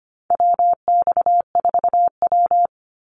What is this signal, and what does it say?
Morse code, heard as a keyed tone, about 25 wpm, 700 Hz. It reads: WX4W
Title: Call Signs Repeated 2x - 25wpm